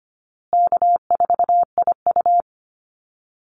Morse code, heard as a keyed tone, about 25 wpm, 700 Hz. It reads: K4IU
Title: Easy Call Signs Repeated 3x - 25wpm